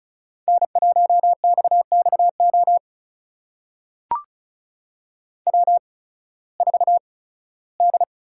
Morse code, heard as a keyed tone, about 36 wpm, spaced as for 19 wpm, 700 Hz. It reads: N1XXO W4D
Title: USA Call Signs - 5 Characters - 8x Character Spacing - 35wpm